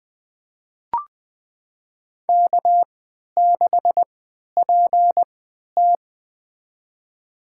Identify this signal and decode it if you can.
Morse code, heard as a keyed tone, about 20 wpm, spaced as for 11 wpm, 700 Hz. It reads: K6PT
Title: USA Call Signs - 4 Characters - 3x Character Spacing - 20wpm